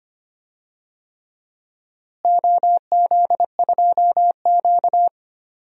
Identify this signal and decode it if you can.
Morse code, heard as a keyed tone, about 25 wpm, 700 Hz. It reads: OZ2Q
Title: DXCC-100 Call Signs - 4 Characters - Repeated 3x - 25wpm